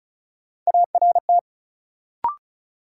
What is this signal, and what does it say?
Morse code, heard as a keyed tone, about 34 wpm, 700 Hz. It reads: ART
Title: Sentences from Top 600 Words - New Word Drill - 35wpm